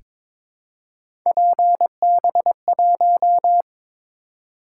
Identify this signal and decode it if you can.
Morse code, heard as a keyed tone, about 22 wpm, 700 Hz. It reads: PB1
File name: practice / 22wpm / Three Letter-Number Combinations - 22wpm